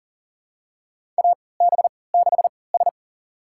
Morse code, heard as a keyed tone, about 42 wpm, spaced as for 23 wpm, 700 Hz. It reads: AB6S
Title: USA Call Signs - 4 Characters - 3x Character Spacing - 40wpm